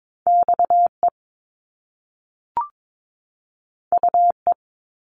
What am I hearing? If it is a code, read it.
Morse code, heard as a keyed tone, about 22 wpm, 700 Hz. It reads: X E U E
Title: Top Two Letter Combinations in Words 22wpm